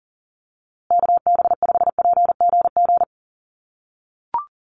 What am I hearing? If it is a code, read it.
Morse code, heard as a keyed tone, about 40 wpm, 700 Hz. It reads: KB5PGG
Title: USA Call Signs Repeated 2x 40wpm